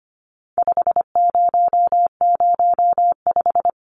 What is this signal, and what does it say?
Morse code, heard as a keyed tone, about 25 wpm, 700 Hz. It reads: 5005
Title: Numbers - 4-Digit 25wpm